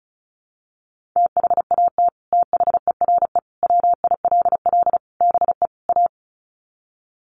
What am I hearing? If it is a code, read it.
Morse code, heard as a keyed tone, about 34 wpm, 700 Hz. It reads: THAT THERE WILL BE A
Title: Sets of 5 Words Encore2 35wpm